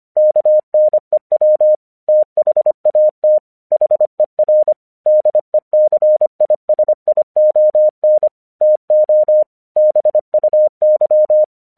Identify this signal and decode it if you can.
Morse code, heard as a keyed tone, about 25 wpm, 610 Hz. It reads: KNEW THAT HER DECISION TO BUY
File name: 1 English 25wpm / AQC27